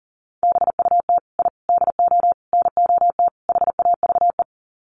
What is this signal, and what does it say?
Morse code, heard as a keyed tone, about 40 wpm, 700 Hz. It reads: BUT I DO NOT HAVE
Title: Sets of 5 Words Encore 40wpm